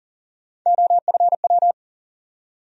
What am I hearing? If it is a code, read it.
Morse code, heard as a keyed tone, about 40 wpm, 700 Hz. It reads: OFW
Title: USA Call Sign Postfixes - 3 Letters 40wpm